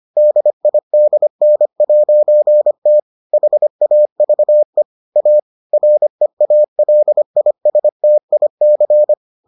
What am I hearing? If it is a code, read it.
Morse code, heard as a keyed tone, about 25 wpm, 600 Hz. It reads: DIDN'T HAVE A REALISTIC